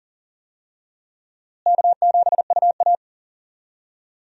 Morse code, heard as a keyed tone, about 40 wpm, 700 Hz. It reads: K7UA
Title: USA Call Signs - 4 Characters - Repeated 2x - 40wpm